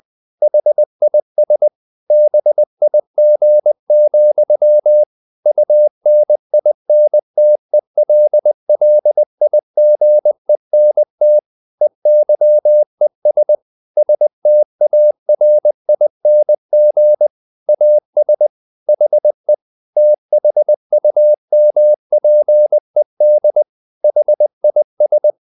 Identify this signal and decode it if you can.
Morse code, heard as a keyed tone, about 20 wpm, 600 Hz. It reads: HIS BIG, UNINTELLIGENT EYES STARING AS HE THUMPED HIS
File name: the old man and the sea / the old man and the sea part 1 20wpm